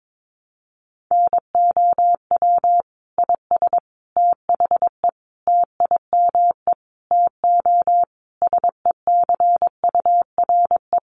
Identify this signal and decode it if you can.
Morse code, heard as a keyed tone, about 22 wpm, 700 Hz. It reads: NOW IS THE TIME TO SECURE
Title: Sentences from Top 1200 Words 22wpm